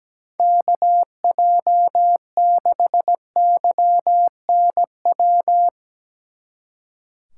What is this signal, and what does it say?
Morse code, heard as a keyed tone, about 18 wpm, 700 Hz. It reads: KJ6YNW
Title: Speed Racing - USA Call Signs 17wpm